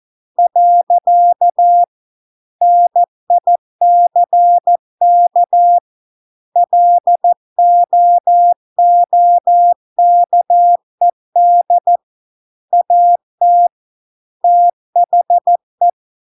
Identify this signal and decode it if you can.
Morse code, heard as a keyed tone, about 14 wpm, 700 Hz. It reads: . NICK LOOKED AT THE